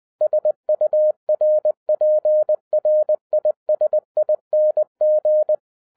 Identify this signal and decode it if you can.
Morse code, heard as a keyed tone, about 20 wpm, 610 Hz. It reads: SURPRISING
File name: 2012-01-mp3 / n05-CAME BACK TO HAM RADIO AFTER 30 YEARS OF INTERVAL-20wpm